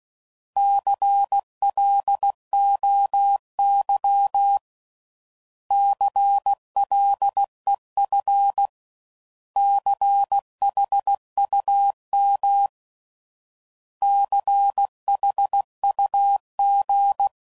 Morse code, heard as a keyed tone, about 16 wpm, 800 Hz. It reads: CLOY CLEF CHUM CHUG
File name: Prefixes / By First Letter / Cs